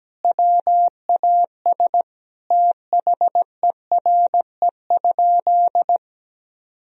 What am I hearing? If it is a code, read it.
Morse code, heard as a keyed tone, about 17 wpm, 700 Hz. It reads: WAS THERE?
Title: Sentences from Top 1000 Words - Review Part 4 - 17wpm